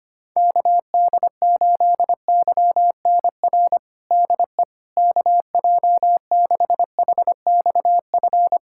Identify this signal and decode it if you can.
Morse code, heard as a keyed tone, about 25 wpm, 700 Hz. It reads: KD8YNR DE KJ6HXF
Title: USA Callsign DE Callsign 25wpm